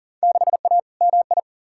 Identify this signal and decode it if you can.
Morse code, heard as a keyed tone, about 40 wpm, 700 Hz. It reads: 6A MI
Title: Field Day Exchanges - 40wpm